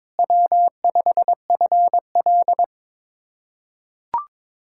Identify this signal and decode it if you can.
Morse code, heard as a keyed tone, about 21 wpm, 700 Hz. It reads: W5FL
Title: USA Callsign DE Callsign - 4 Characters - 22wpm